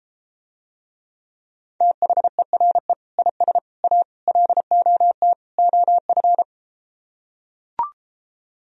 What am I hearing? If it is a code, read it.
Morse code, heard as a keyed tone, about 33 wpm, 700 Hz. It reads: THERE IS A LOT OF